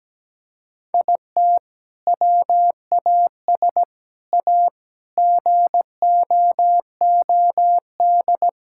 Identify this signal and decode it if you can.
Morse code, heard as a keyed tone, about 17 wpm, 700 Hz. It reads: IT WAS A GOOD
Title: Sentences from Top 1000 Words - Review Part 5 - 17wpm